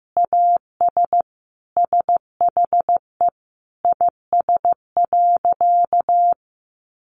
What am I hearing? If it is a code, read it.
Morse code, heard as a keyed tone, about 15 wpm, 700 Hz. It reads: AS SHE IS.
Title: Sentences from Top 1300 Words 15wpm